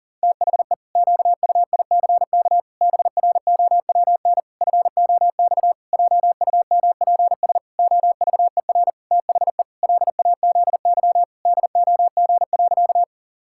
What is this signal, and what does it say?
Morse code, heard as a keyed tone, about 40 wpm, 700 Hz. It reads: THE QUICK BROWN FOX JUMPS OVER THE LAZY DOG.
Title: Pangram Challenge v2 - 40wpm